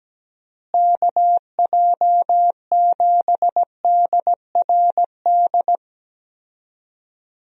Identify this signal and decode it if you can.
Morse code, heard as a keyed tone, about 17 wpm, 700 Hz. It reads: KJ7DRD
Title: USA Call Signs - 6 Characters - Repeated 2x - 17wpm